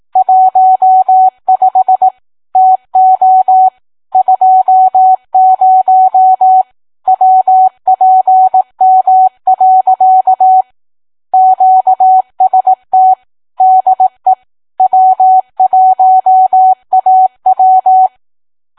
Morse code, heard as a keyed tone, about 18 wpm, 750 Hz. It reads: 15 TO 20 WPM. QST DE W1AW